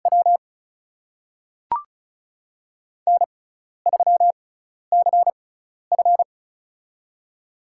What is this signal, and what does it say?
Morse code, heard as a keyed tone, about 36 wpm, spaced as for 12 wpm, 700 Hz. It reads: W N3CF
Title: USA Call Signs - 4 Characters - 6x Character Spacing - 35wpm